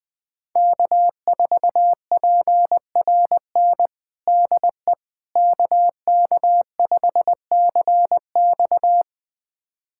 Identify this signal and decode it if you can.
Morse code, heard as a keyed tone, about 21 wpm, 700 Hz. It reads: K4PRN DE KK5CX
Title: USA Callsign DE Callsign - 5 Characters - 20wpm